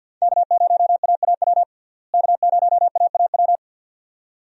Morse code, heard as a keyed tone, about 51 wpm, 700 Hz. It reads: K0AAW K0AAW
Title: USA Call Signs - 5 Characters - Repeated 2x - 50wpm